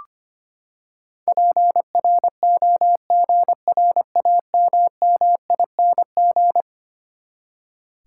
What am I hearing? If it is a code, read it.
Morse code, heard as a keyed tone, about 25 wpm, 700 Hz. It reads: PROGRAMMING
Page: morse-code-ninja-practice-25wpm